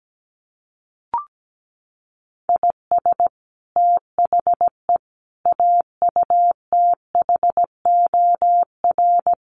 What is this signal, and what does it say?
Morse code, heard as a keyed tone, about 17 wpm, 700 Hz. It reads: IS THE AUTHOR
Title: Sets of 4 Words 17wpm